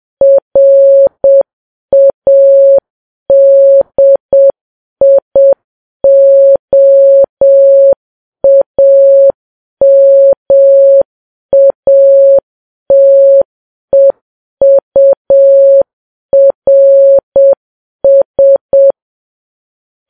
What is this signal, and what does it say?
Morse code, heard as a keyed tone, about 7 wpm, 550 Hz. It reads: RADIOAMATEURS
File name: Files Web tls / Morse on9fv / morse7kps